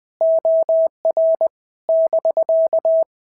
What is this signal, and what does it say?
Morse code, heard as a keyed tone, about 20 wpm, 650 Hz. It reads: OR <BK>
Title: POTA QSO Practice - 20wpm